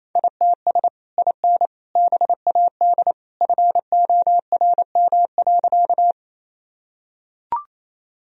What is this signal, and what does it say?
Morse code, heard as a keyed tone, about 28 wpm, 700 Hz. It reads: ITS IN BAD FORM.
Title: Sentences from Top 1000 Words - Review Part 1 - 28wpm